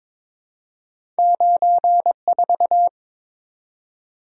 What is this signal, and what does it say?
Morse code, heard as a keyed tone, about 23 wpm, 700 Hz. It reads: 94
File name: Two Letter-Number Combinations - 22wpm